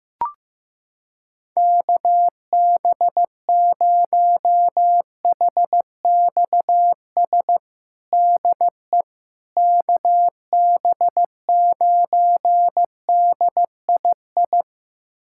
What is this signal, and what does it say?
Morse code, heard as a keyed tone, about 16 wpm, 700 Hz. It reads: KB0HXS DE KB9DII
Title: USA Callsign DE Callsign - 6 Characters - 15wpm